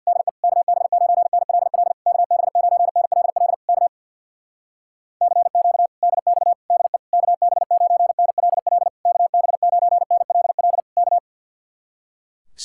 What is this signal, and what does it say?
Morse code, heard as a keyed tone, about 63 wpm, 700 Hz. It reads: DE KB9NLL KB9NLL K CQ DX DE KB9NLL KB9NLL K